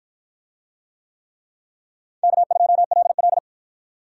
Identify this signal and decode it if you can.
Morse code, heard as a keyed tone, about 53 wpm, 700 Hz. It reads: K2LL